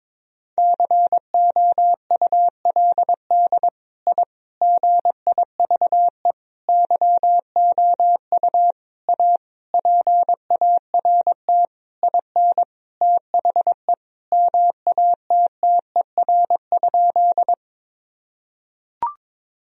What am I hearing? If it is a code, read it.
Morse code, heard as a keyed tone, about 22 wpm, 700 Hz. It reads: COULD I GIVE YOU A PART IN THE MATTER?
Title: Sentences from Top 1000 Words - Review Part 1 - 22wpm